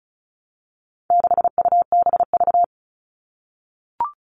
Morse code, heard as a keyed tone, about 34 wpm, 700 Hz. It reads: 6UBV
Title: All Cut Numbers - 4-Digit 35wpm